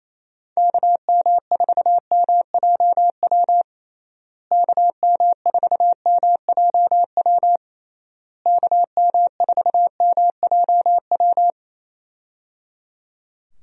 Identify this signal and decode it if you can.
Morse code, heard as a keyed tone, about 28 wpm, 700 Hz. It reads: KM4MJW KM4MJW KM4MJW
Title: USA Call Signs - 6 Characters - Repeated 3x - 28wpm